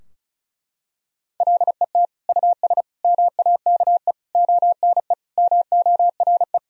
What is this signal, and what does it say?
Morse code, heard as a keyed tone, about 36 wpm, 700 Hz. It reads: LET US MAKE ONE MORE
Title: Sentences from Top 1000 Words - Review Part 4 - 35wpm